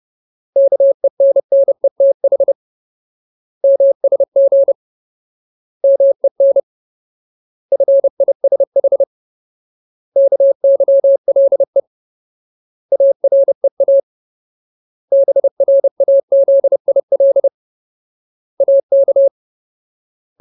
Morse code, heard as a keyed tone, about 30 wpm, 550 Hz. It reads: KENNETH MSG MEN FISH KYLE AREA BRAZIL AK
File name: Ultimate Word List / 30WPM Ultimate